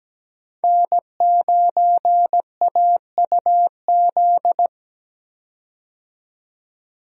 Morse code, heard as a keyed tone, about 17 wpm, 700 Hz. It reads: N9AUZ
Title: USA Call Signs - 5 Characters - 17wpm